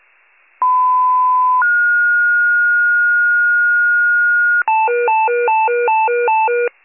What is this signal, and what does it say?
Two-tone page, heard as a one-second tone, then a three-second tone, about 1010 then 1430 Hz. It is heard from 0.6 to 4.6 s.
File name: wcalert